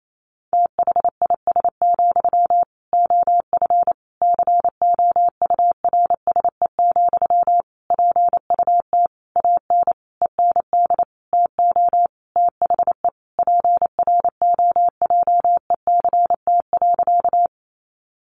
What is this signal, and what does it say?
Morse code, heard as a keyed tone, about 28 wpm, 700 Hz. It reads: THIS, OF COURSE, PUT AN END TO THE PROJECT.
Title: Sentences from Top 1400 Words 28wpm